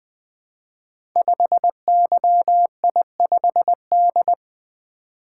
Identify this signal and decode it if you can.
Morse code, heard as a keyed tone, about 21 wpm, 700 Hz. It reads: 5YI5D
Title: Code Groups - 5 Letter:Numbers - 20wpm